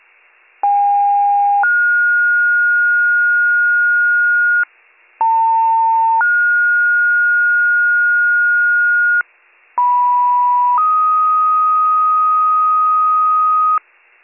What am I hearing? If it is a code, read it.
Two-tone page, heard as a one-second tone, then a three-second tone, about 810 then 1430 Hz, about 900 then 1430 Hz, about 980 then 1250 Hz, with noise. It is heard from 0.6 to 13.8 s.